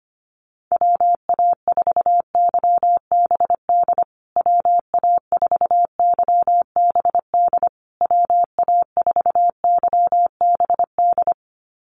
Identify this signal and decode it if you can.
Morse code, heard as a keyed tone, about 25 wpm, 700 Hz. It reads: WA4YBD WA4YBD WA4YBD
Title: USA Call Signs Repeated 3x 25wpm